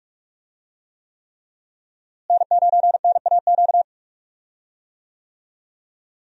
Morse code, heard as a keyed tone, about 42 wpm, 700 Hz. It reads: N9NAQ
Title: USA Call Signs - 5 Characters - Repeated 3x - 45wpm